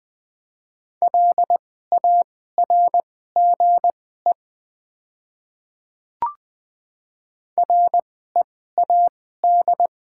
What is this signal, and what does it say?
Morse code, heard as a keyed tone, about 20 wpm, spaced as for 14 wpm, 700 Hz. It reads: LARGE READ
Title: 2x Character Spacing - Top 200 Words - 20wpm